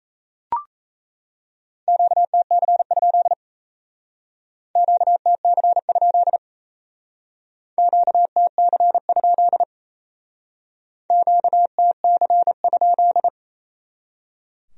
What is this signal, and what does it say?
Morse code, heard as a keyed tone, about 35 wpm, 700 Hz. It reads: QTC? QTC? QTC? QTC?